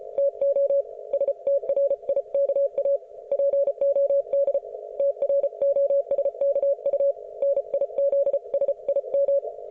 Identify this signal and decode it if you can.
Morse code, heard as a keyed tone, about 34 wpm, 560 Hz. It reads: TO STRIKA POD TROSKU NIZSIM